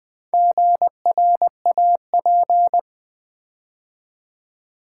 Morse code, heard as a keyed tone, about 20 wpm, 700 Hz. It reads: GRAP
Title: Build A Word - 2 Letters At A Time - 8-Letter Words - 20wpm